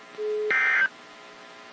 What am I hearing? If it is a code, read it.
Emergency Alert System (SAME) tones (end-of-message bursts), heard as harsh buzzing bursts. It is heard from 0.5 to 0.8 s.
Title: WZ2523 - NOAA Weather Radio